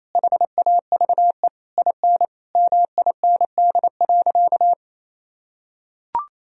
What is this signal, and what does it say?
Morse code, heard as a keyed tone, about 27 wpm, 700 Hz. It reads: HAVE IN MIND.
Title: Sentences from Top 1000 Words - Review Part 3 - 28wpm